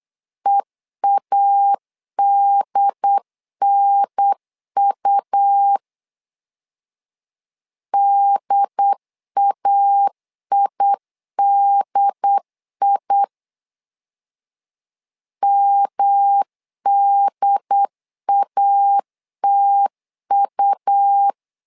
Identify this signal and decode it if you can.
Morse code, heard as a keyed tone, about 8 wpm, 800 Hz. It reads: EADNU DAIDI MDATU
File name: LEZIONI / 3 MWRO / 3 MW 8wpm 2501